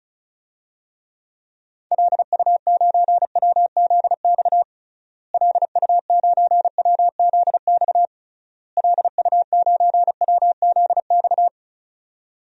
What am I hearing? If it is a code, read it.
Morse code, heard as a keyed tone, about 34 wpm, 700 Hz. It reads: LU9WZX LU9WZX LU9WZX
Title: DXCC-100 Call Signs - 6 Characters - Repeated 3x - 35wpm